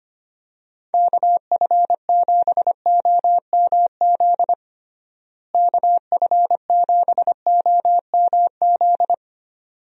Morse code, heard as a keyed tone, about 25 wpm, 700 Hz. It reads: KF7OMZ KF7OMZ
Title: USA Call Signs - 6 Characters - Repeated 2x - 25wpm